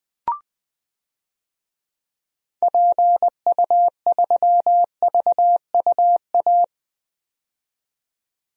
Morse code, heard as a keyed tone, about 21 wpm, 700 Hz. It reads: PU3VUA
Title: DXCC-100 Call Signs - 6 Characters - 20wpm